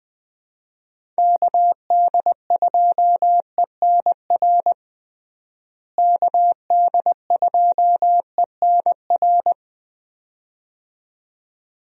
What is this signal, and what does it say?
Morse code, heard as a keyed tone, about 21 wpm, 700 Hz. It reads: KD2ENR KD2ENR
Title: USA Call Signs - 6 Characters - Repeated 2x - 20wpm